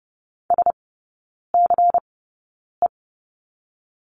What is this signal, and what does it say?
Morse code, heard as a keyed tone, about 30 wpm, 700 Hz. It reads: S C E
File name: Lesson 19e - Letter w - Three Characters 30wpm